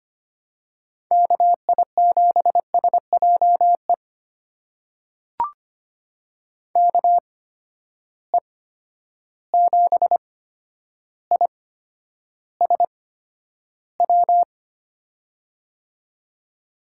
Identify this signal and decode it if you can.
Morse code, heard as a keyed tone, about 25 wpm, spaced as for 11 wpm, 700 Hz. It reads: KI7SJE KE7ISW
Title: USA Call Signs - 6 Characters - 8x Character Spacing - 25wpm